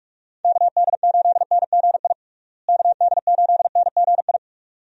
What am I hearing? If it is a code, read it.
Morse code, heard as a keyed tone, about 42 wpm, 700 Hz. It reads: KD8NGI KD8NGI
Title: USA Call Signs - 6 Characters - Repeated 2x - 45wpm